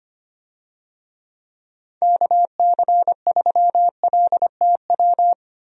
Morse code, heard as a keyed tone, about 25 wpm, 700 Hz. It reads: KC3LTW
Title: USA Call Signs - 6 Characters - Repeated 3x - 25wpm